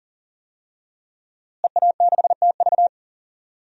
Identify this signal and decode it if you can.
Morse code, heard as a keyed tone, about 40 wpm, 700 Hz. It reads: EA6TV
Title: DXCC-100 Call Signs - 5 Characters - Repeated 2x - 40wpm